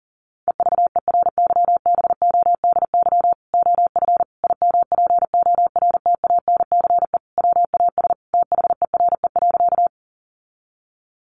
Morse code, heard as a keyed tone, about 40 wpm, 700 Hz. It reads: EVERYBODY OF IMPORTANCE WAS THERE.
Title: Sentences from Top 1400 Words 40wpm